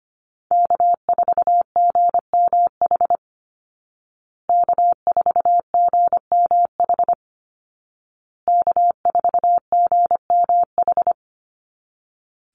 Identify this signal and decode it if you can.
Morse code, heard as a keyed tone, about 25 wpm, 700 Hz. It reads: K4GMH K4GMH K4GMH
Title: Call Signs Repeated 2x - 25wpm